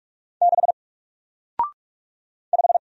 Morse code, heard as a keyed tone, about 45 wpm, 700 Hz. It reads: 6 5
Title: Single Letter-Number Alt 45wpm